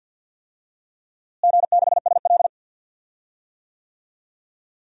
Morse code, heard as a keyed tone, about 52 wpm, 700 Hz. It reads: G6SL
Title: DXCC-100 Call Signs - 4 Characters - Repeated 2x - 50wpm